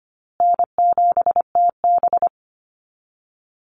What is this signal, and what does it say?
Morse code, heard as a keyed tone, about 25 wpm, 700 Hz. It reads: N7TB
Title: Easy Call Signs Repeated 3x - 25wpm